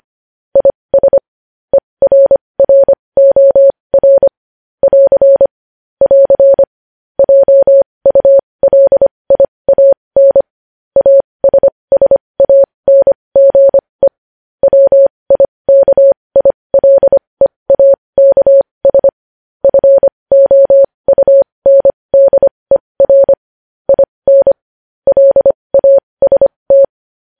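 Morse code, heard as a keyed tone, about 25 wpm, spaced as for 20 wpm, 550 Hz. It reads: IS ERROR <AR> <AR> JULIAN ASSANGE WIKILEAKS FOUNDER IN LAST
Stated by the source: News headlines sent in Morse code at 20.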